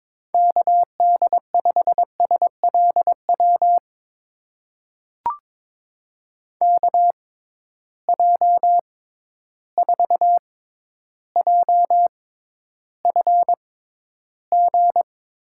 Morse code, heard as a keyed tone, about 22 wpm, spaced as for 11 wpm, 700 Hz. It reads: KD5SLW KJ4JFG
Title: USA Call Signs - 6 Characters - 6x Character Spacing - 22wpm